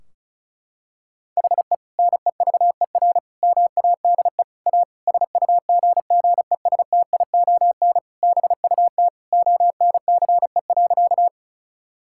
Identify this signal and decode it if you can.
Morse code, heard as a keyed tone, about 34 wpm, 700 Hz. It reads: HE NEVER MADE A SUGGESTION BUT ONCE.
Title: Sentences from Top 1600 Words 35wpm